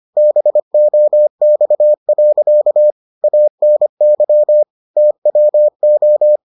Morse code, heard as a keyed tone, about 25 wpm, 600 Hz. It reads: BOX. ANY TWO